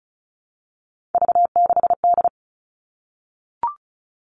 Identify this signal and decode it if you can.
Morse code, heard as a keyed tone, about 34 wpm, 700 Hz. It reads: V6D
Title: All Cut Numbers - 3-Digit 35wpm